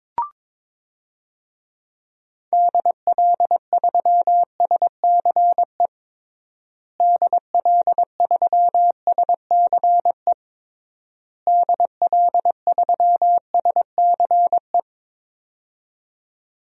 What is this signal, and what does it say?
Morse code, heard as a keyed tone, about 22 wpm, 700 Hz. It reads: DL3SCE DL3SCE DL3SCE
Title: DXCC-100 Call Signs - 6 Characters - Repeated 3x - 22wpm